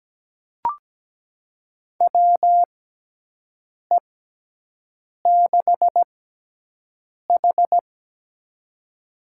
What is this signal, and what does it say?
Morse code, heard as a keyed tone, about 17 wpm, spaced as for 6 wpm, 700 Hz. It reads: WE6H
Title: USA Call Signs - 4 Characters - 6x Character Spacing - 17wpm